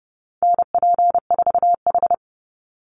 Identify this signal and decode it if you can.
Morse code, heard as a keyed tone, about 30 wpm, 700 Hz. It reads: NP4H
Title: Call Signs Repeated 3x - 30wpm